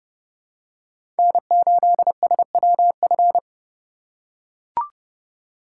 Morse code, heard as a keyed tone, about 32 wpm, 700 Hz. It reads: N8SWF
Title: USA Call Signs - 4x Character Spacing - 30wpm